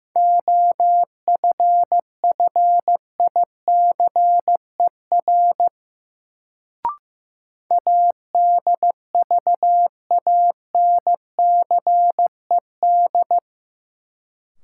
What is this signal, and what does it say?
Morse code, heard as a keyed tone, about 15 wpm, 700 Hz. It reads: OFFICER ADVANCED
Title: Sentences from Top 1200 Words - New Word Drill Alt - 15wpm